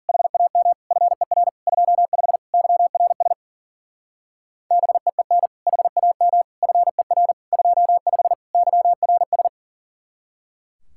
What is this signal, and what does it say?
Morse code, heard as a keyed tone, about 49 wpm, 700 Hz. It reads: HAM FER 25 YRS BEEN HAM FER 25 YRS
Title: Speed-Racing Common QSO Phrases - 40wpm